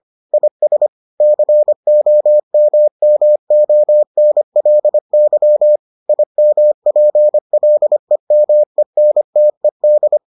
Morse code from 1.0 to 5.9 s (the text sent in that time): COMMONLY